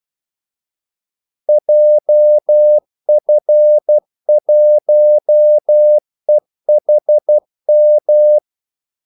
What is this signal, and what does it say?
Morse code, heard as a keyed tone, about 12 wpm, 600 Hz. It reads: JF1EHM
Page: 2 ekstra ordmellemrum mellem kaldesignalerne